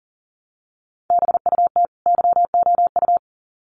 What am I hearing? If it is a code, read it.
Morse code, heard as a keyed tone, about 40 wpm, 700 Hz. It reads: BUT YOU
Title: Sets of 2 Words 40wpm